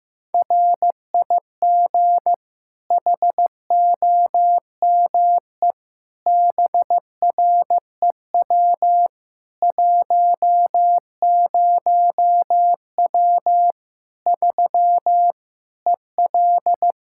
Morse code, heard as a keyed tone, about 15 wpm, 700 Hz. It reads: RIG HOME BREW 10W 3 EL
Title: Practice QSO - Exchange 2 - Info - 15wpm